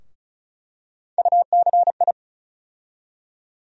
Morse code, heard as a keyed tone, about 36 wpm, 700 Hz. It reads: UCI
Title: USA Call Sign Postfixes - 3 Letters 35wpm